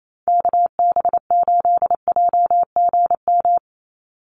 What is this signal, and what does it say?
Morse code, heard as a keyed tone, about 28 wpm, 700 Hz. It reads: KB8JGM
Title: USA Call Signs 28wpm